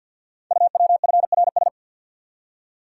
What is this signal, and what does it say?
Morse code, heard as a keyed tone, about 50 wpm, 700 Hz. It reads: UWFRS
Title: Code Groups - 5 Letters - 50wpm